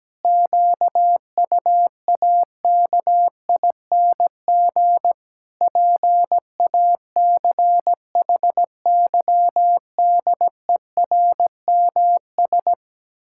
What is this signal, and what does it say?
Morse code, heard as a keyed tone, about 17 wpm, 700 Hz. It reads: QUAKING PACHYDERMS
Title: Pangram Challenge v2 - 17wpm